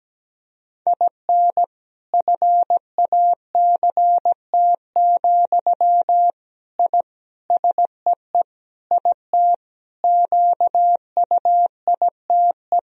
Morse code, heard as a keyed tone, about 17 wpm, 700 Hz. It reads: IN FACT, I SEE IT QUITE
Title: Sentences from Top 1000 Words - Review Part 3 - 17wpm